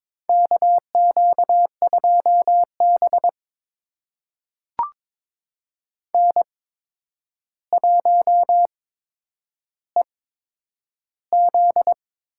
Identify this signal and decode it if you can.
Morse code, heard as a keyed tone, about 22 wpm, spaced as for 9 wpm, 700 Hz. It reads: KQ2B N1EZ
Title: USA Call Signs - 4 Characters - 8x Character Spacing - 22wpm